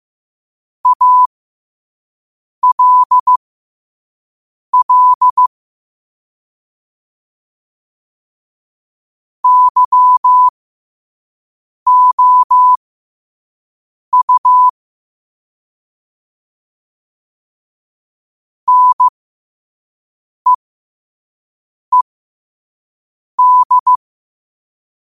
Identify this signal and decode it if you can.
Morse code, heard as a keyed tone, about 15 wpm, spaced as for 5 wpm, 1000 Hz. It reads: ALL YOU NEED
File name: qotdfiles / 5wpm